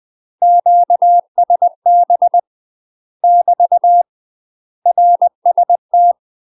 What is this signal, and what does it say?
Morse code, heard as a keyed tone, about 20 wpm, 700 Hz. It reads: QSB <BT> RST
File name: UK MASTER QSOs TRAINING / 20WPM / UK MASTER QSOs 0058